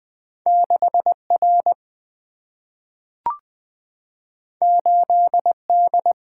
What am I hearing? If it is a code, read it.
Morse code, heard as a keyed tone, about 20 wpm, 700 Hz. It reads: 6 R 8 D
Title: Single-Letter Single-Number 20wpm